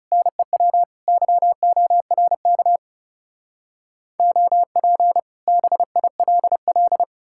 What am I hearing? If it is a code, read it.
Morse code, heard as a keyed tone, about 40 wpm, 700 Hz. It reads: NEW YORK OP BILL